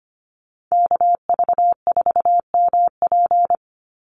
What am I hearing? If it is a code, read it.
Morse code, heard as a keyed tone, about 25 wpm, 700 Hz. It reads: KV4MP
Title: USA Call Signs 25wpm